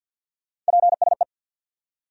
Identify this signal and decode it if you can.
Morse code, heard as a keyed tone, about 69 wpm, spaced as for 59 wpm, 700 Hz. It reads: PSE
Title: Speed-Racing Abbreviations - 50wpm